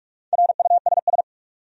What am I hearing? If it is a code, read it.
Morse code, heard as a keyed tone, about 46 wpm, 700 Hz. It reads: RUSS
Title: USA Common Names - Max 5 Letters - 45wpm